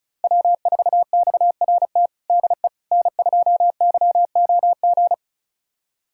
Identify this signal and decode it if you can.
Morse code, heard as a keyed tone, about 36 wpm, 700 Hz. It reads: W4XRT DE N2YOG
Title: USA Callsign DE Callsign 35wpm